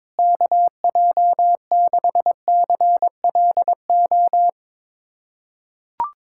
Morse code, heard as a keyed tone, about 22 wpm, 700 Hz. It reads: KJ6CLO
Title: USA Callsign DE Callsign - 6 Characters - 22wpm